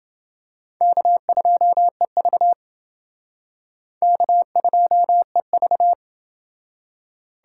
Morse code, heard as a keyed tone, about 29 wpm, 700 Hz. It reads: K2EV K2EV